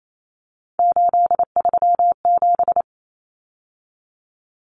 Morse code, heard as a keyed tone, about 28 wpm, 700 Hz. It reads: 837
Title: Numbers - 3-Digit 28wpm